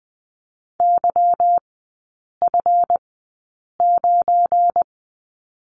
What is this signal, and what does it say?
Morse code, heard as a keyed tone, about 20 wpm, spaced as for 8 wpm, 700 Hz. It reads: YF9
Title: 172 Lesson 28e - Number 9 - Three Characters 20wpm